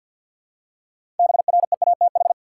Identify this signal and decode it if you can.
Morse code, heard as a keyed tone, about 49 wpm, 700 Hz. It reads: BREATH
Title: Six Letter Words - 50wpm